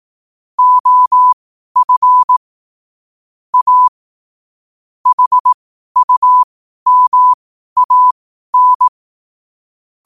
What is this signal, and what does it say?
Morse code, heard as a keyed tone, about 18 wpm, spaced as for 12 wpm, 1000 Hz. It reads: OF A HUMAN
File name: qotdfiles / 12wpm